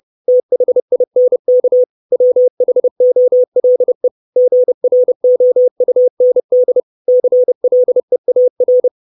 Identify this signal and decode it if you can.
Morse code, heard as a keyed tone, about 30 wpm, 500 Hz. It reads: THINK WHOLE GROUND CLEAR
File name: 632 Words 30WPM